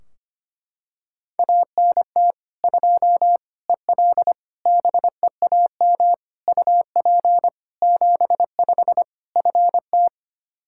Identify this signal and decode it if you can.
Morse code, heard as a keyed tone, about 25 wpm, 700 Hz. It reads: ANT 2 EL BEAM UP 75 FT
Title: Practice QSO - Exchange 2 - Info - 25wpm